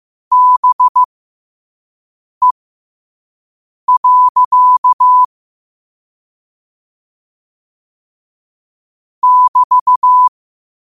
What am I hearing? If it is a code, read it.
Morse code, heard as a keyed tone, about 15 wpm, spaced as for 5 wpm, 1000 Hz. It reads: BE. <BT>
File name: qotdfiles / 5wpm